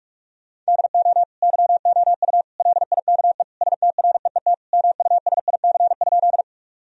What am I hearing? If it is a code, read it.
Morse code, heard as a keyed tone, about 46 wpm, 700 Hz. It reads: DO YOU LIKE STREET MUSIC?
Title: Sentences from Top 1000 Words - Review Part 3 - 45wpm